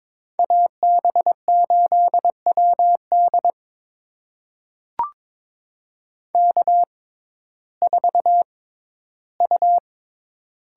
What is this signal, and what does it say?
Morse code, heard as a keyed tone, about 22 wpm, spaced as for 13 wpm, 700 Hz. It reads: AB8WD K4U
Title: USA Call Signs - 5 Characters - 6x Character Spacing - 22wpm